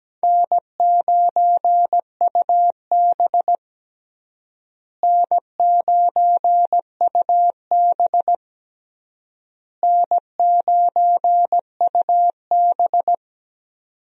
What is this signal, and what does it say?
Morse code, heard as a keyed tone, about 17 wpm, 700 Hz. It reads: N9UB N9UB N9UB
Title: USA Call Signs - 4 Characters - Repeated 3x - 17wpm